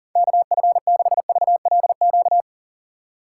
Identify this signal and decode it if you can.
Morse code, heard as a keyed tone, about 37 wpm, 700 Hz. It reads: KF6VLQ
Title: USA Call Signs - 6 Characters - 2x Character Spacing - 40wpm